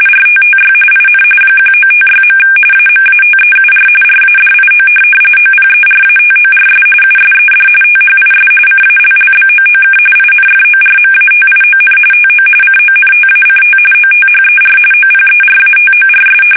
RTTY, about 45.45 baud. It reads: RTTY IN 1956. AMATEUR RADIO OPERATORS USED THIS CALLBOOK INFORMATION TO CONTACT OTHER OPERATORS
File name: fsk850 / teletype-RTTY